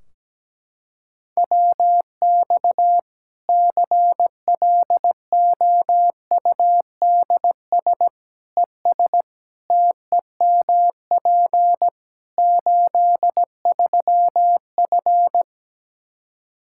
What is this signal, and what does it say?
Morse code, heard as a keyed tone, about 17 wpm, 700 Hz. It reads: WX CLOUDS ES TEMP 83F
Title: Practice QSO - Exchange 2 - Info - 17wpm